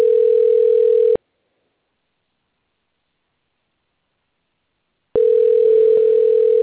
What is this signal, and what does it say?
Ringback tone (North America), 440 + 480 Hz.